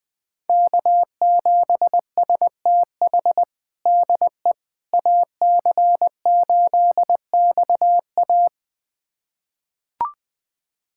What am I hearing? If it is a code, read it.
Morse code, heard as a keyed tone, about 20 wpm, 700 Hz. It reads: K7STH DE AC8XA
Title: USA Callsign DE Callsign - 5 Characters - 20wpm